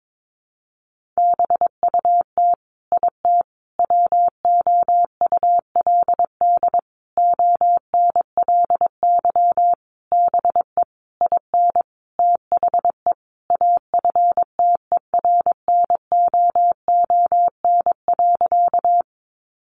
Morse code, heard as a keyed tone, about 22 wpm, 700 Hz. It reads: BUT IT WOULD ONLY BE IN THE AFTERNOON.
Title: Sentences from Top 1500 Words 22wpm